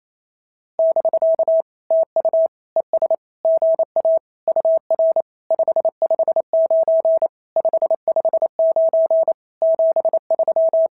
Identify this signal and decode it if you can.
Morse code, heard as a keyed tone, about 27 wpm, 650 Hz. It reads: <BK> TU ES GA UR 559 559 73
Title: SOTA QSO Practice 28wpm